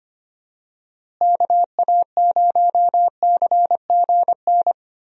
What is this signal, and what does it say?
Morse code, heard as a keyed tone, about 25 wpm, 700 Hz. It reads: KA0CGN
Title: USA Call Signs - 6 Characters - Repeated 3x - 25wpm